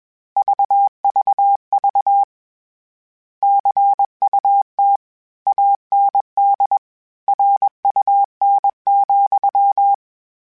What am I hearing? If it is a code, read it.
Morse code, heard as a keyed tone, about 21 wpm, 800 Hz. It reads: VVV CUT AND RUN,
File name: Misc / Misc-mp3s / Word Pairs @20